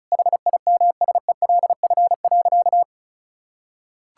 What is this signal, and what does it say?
Morse code, heard as a keyed tone, about 34 wpm, 700 Hz. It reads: HIMSELF.
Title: Sentences from Top 1600 Words 35wpm